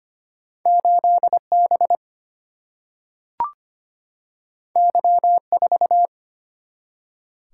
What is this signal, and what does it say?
Morse code, heard as a keyed tone, about 25 wpm, 700 Hz. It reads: 8B Y4
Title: Single-Letter Single-Number 25wpm